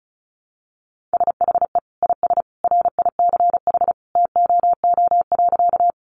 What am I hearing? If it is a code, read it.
Morse code, heard as a keyed tone, about 34 wpm, 700 Hz. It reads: SHE IS RICH TOO.
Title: Sentences from Top 1100 Words 35wpm